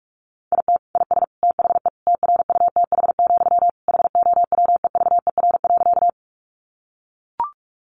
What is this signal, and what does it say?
Morse code, heard as a keyed tone, about 44 wpm, 700 Hz. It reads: IT IS THE TRUTH, HOWEVER.
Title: Sentences from Top 800 Words 45wpm